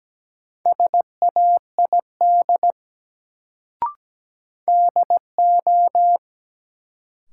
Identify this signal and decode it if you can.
Morse code, heard as a keyed tone, about 17 wpm, 700 Hz. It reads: SAID DO
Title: Sentences from Top 100 Words - New Word Drill - 17wpm